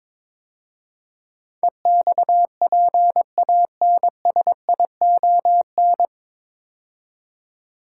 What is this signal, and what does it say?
Morse code, heard as a keyed tone, about 23 wpm, 700 Hz. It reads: EXPANSION
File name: 9 Letter Words - First 3 Letters Given - 22wpm